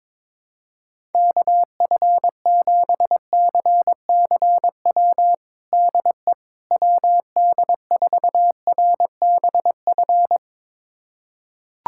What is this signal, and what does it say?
Morse code, heard as a keyed tone, about 22 wpm, 700 Hz. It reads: KF7CCW DE WD4RBF
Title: USA Callsign DE Callsign - 6 Characters - 22wpm